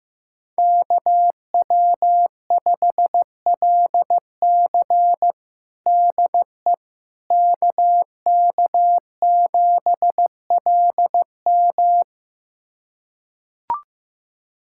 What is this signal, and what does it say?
Morse code, heard as a keyed tone, about 15 wpm, 700 Hz. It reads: KW5LC DE KK7LM
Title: USA Callsign DE Callsign - 5 Characters - 15wpm